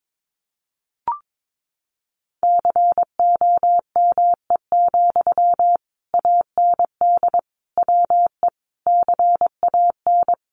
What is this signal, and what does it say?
Morse code, heard as a keyed tone, about 22 wpm, 700 Hz. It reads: COME, AND WE CAN
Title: Sentences from Top 500 Words 22wpm